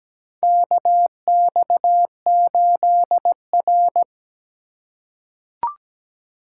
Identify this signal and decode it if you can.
Morse code, heard as a keyed tone, about 17 wpm, 700 Hz. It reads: KX8R
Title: USA Call Signs - 2x Character Spacing - 17wpm